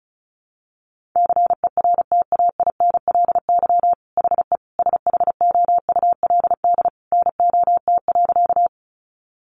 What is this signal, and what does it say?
Morse code, heard as a keyed tone, about 34 wpm, 700 Hz. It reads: CERTAINLY HE SHOULD NOT.
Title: Sentences from Top 900 Words 35wpm